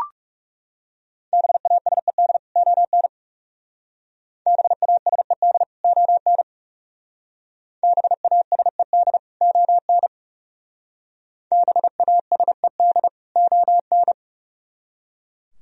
Morse code, heard as a keyed tone, about 39 wpm, 700 Hz. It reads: BASED ON BASED ON BASED ON BASED ON